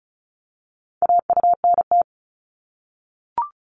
Morse code, heard as a keyed tone, about 36 wpm, 700 Hz. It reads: AUNT
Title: Multi-letter Phonemes Repeated 3x 35wpm